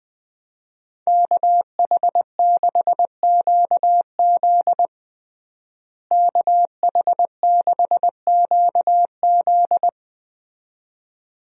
Morse code, heard as a keyed tone, about 20 wpm, 700 Hz. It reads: KH6QZ KH6QZ
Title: USA Call Signs - 5 Characters - Repeated 2x - 20wpm